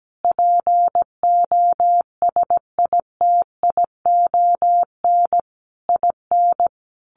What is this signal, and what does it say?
Morse code, heard as a keyed tone, about 17 wpm, 700 Hz. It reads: POSITION IN
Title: Sentences from Top 1300 Words 17wpm